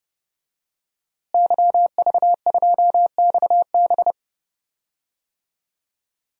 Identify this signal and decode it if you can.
Morse code, heard as a keyed tone, about 32 wpm, 700 Hz. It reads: YV2XB
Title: DXCC-100 Call Signs - 5 Characters - Repeated 2x - 30wpm